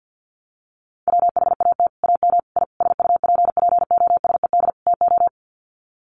Morse code, heard as a keyed tone, about 48 wpm, 700 Hz. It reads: WHAT AM I SUPPOSED TO
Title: Sets of 5 Words Encore2 50wpm